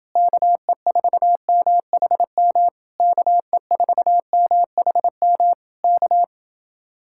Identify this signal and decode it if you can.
Morse code, heard as a keyed tone, about 29 wpm, 700 Hz. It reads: KE4MHM KE4MHM K